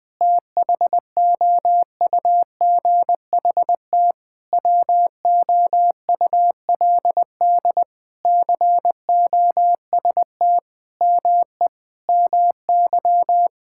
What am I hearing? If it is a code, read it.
Morse code, heard as a keyed tone, about 20 wpm, 700 Hz. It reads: THOUGHT WOULD COST ME MY
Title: Sentences from Top 1000 Words - Review Part 5 - 20wpm